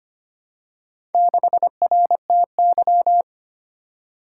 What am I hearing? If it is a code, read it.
Morse code, heard as a keyed tone, about 25 wpm, 700 Hz. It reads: 6RTY
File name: practice / 25wpm / Four Letter-Number Combinations - 25wpm